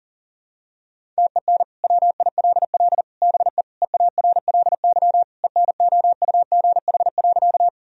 Morse code, heard as a keyed tone, about 40 wpm, 700 Hz. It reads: TEN WILL BE EARLY ENOUGH.
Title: Sentences from Top 1000 Words - Review Part 2 - 40wpm